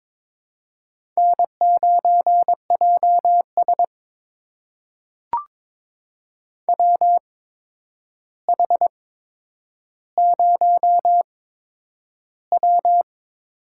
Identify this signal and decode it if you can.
Morse code, heard as a keyed tone, about 22 wpm, spaced as for 9 wpm, 700 Hz. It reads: N9JS WH0W
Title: USA Call Signs - 4 Characters - 8x Character Spacing - 22wpm